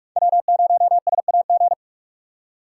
Morse code, heard as a keyed tone, about 42 wpm, 700 Hz. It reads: W0SAG